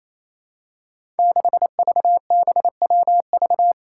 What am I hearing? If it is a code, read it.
Morse code, heard as a keyed tone, about 28 wpm, 700 Hz. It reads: 6VBWV
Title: Five Challenging Letter-Number Combinations 28wpm